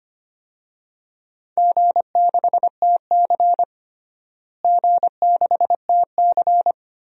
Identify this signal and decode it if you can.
Morse code, heard as a keyed tone, about 25 wpm, 700 Hz. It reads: G6TC G6TC
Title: DXCC-100 Call Signs - 4 Characters - Repeated 2x - 25wpm